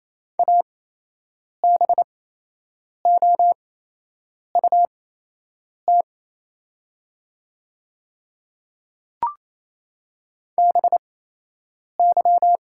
Morse code, heard as a keyed tone, about 27 wpm, spaced as for 8 wpm, 700 Hz. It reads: ABOUT BY
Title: 8x Character Spacing - Top 100 Words - 28wpm